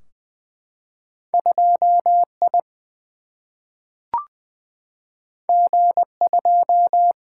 Morse code, heard as a keyed tone, about 20 wpm, 700 Hz. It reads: 2 I G 2
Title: Single-Letter Single-Number 20wpm